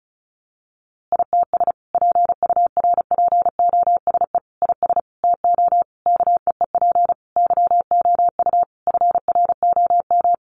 Morse code, heard as a keyed tone, about 34 wpm, 700 Hz. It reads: ITS PURPOSE IS TO KEEP YOU FROM
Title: A Sound Of Thunder 35wpm